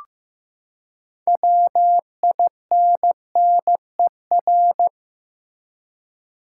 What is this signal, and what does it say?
Morse code, heard as a keyed tone, about 15 wpm, 700 Hz. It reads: WINNER
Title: Six Letter Words - 15wpm